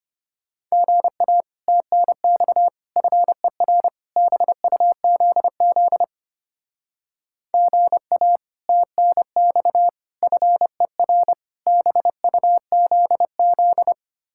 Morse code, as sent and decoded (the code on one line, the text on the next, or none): --. .- / - -. -..- / ..-. .-. / -... ..- --.. --.. / --. .- / - -. -..- / ..-. .-. / -... ..- --.. --..
GA TNX FER BUZZ GA TNX FER BUZZ